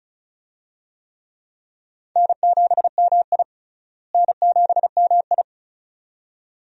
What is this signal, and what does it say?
Morse code, heard as a keyed tone, about 34 wpm, 700 Hz. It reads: N7MI N7MI
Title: USA Call Signs - 4 Characters - Repeated 2x - 35wpm